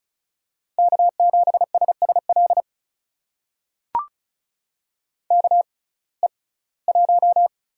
Morse code, heard as a keyed tone, about 36 wpm, spaced as for 22 wpm, 700 Hz. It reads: K7SSL KE1
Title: USA Call Signs - 5 Characters - 6x Character Spacing - 35wpm